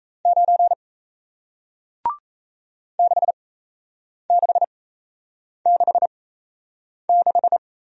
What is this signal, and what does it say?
Morse code, heard as a keyed tone, about 38 wpm, spaced as for 8 wpm, 700 Hz. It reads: 9 6666